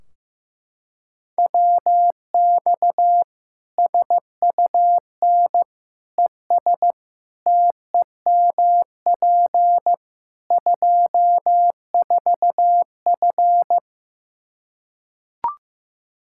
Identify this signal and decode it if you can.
Morse code, heard as a keyed tone, about 15 wpm, 700 Hz. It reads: WX SUN ES TEMP 24F
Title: Practice QSO - Exchange 2 - Info - 15wpm